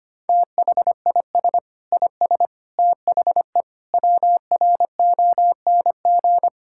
Morse code, heard as a keyed tone, about 25 wpm, 700 Hz. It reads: THIS IS THE WRONG
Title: Sentences from Top 1000 Words - Review Part 6 - 25wpm